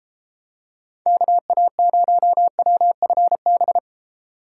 Morse code, heard as a keyed tone, about 32 wpm, 700 Hz. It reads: KA0WFB